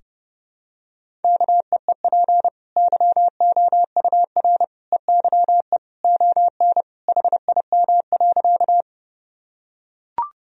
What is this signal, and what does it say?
Morse code, heard as a keyed tone, about 30 wpm, 700 Hz. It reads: KEEP YOUR EYE ON HIM.
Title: Sentences from Top 1000 Words - Review Part 1 - 30wpm